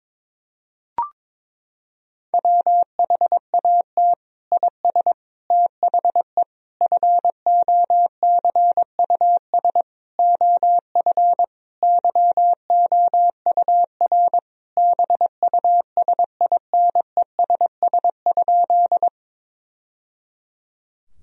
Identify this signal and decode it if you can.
Morse code, heard as a keyed tone, about 22 wpm, 700 Hz. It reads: WHAT IS THE FOCUS OF YOUR BUSINESS?
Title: Sentences from Top 1000 Words - Review Part 4 - 22wpm